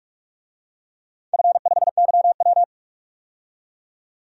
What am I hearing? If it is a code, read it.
Morse code, heard as a keyed tone, about 43 wpm, 700 Hz. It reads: F5YW
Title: DXCC-100 Call Signs - 4 Characters - 45wpm